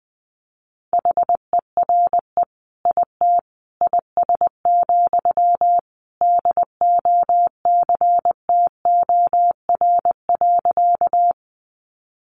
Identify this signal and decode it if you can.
Morse code, heard as a keyed tone, about 20 wpm, 700 Hz. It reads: HERE IT IS, DOCTOR.
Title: Sentences from Top 1500 Words 20wpm